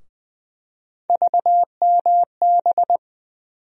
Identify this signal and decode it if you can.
Morse code, heard as a keyed tone, about 21 wpm, 700 Hz. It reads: VMB
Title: USA Call Sign Postfixes - 3 Letters 20wpm